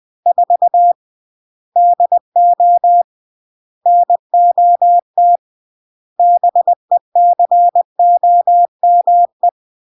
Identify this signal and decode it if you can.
Morse code, heard as a keyed tone, about 20 wpm, 700 Hz. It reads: 4 DO NOT BECOME
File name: CW / jodorowsky / 82 commandments 20-20